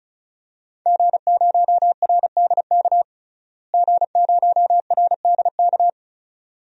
Morse code, heard as a keyed tone, about 36 wpm, 700 Hz. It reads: G0RDK G0RDK
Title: DXCC-100 Call Signs - 5 Characters - Repeated 2x - 35wpm